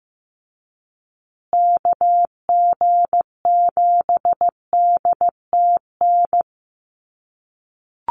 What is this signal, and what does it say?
Morse code, heard as a keyed tone, about 15 wpm, 700 Hz. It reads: KG7DTN
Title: USA Call Signs Repeated 2x 15wpm